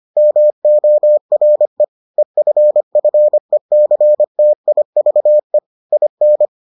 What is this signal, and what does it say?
Morse code, heard as a keyed tone, about 25 wpm, 600 Hz. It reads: MORE EFFECTIVE IN